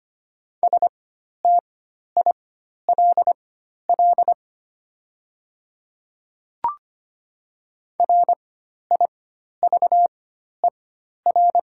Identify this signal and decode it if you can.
Morse code, heard as a keyed tone, about 25 wpm, spaced as for 12 wpm, 700 Hz. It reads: STILL RIVER
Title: 4x Character Spacing - Top 500 Words - 25wpm